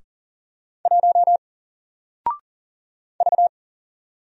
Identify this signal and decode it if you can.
Morse code, heard as a keyed tone, about 40 wpm, 700 Hz. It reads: 1 V
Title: Single Letter-Number 40wpm